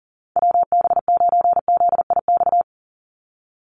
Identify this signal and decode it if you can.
Morse code, heard as a keyed tone, about 40 wpm, 700 Hz. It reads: WB9ZIX
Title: USA Call Signs 40wpm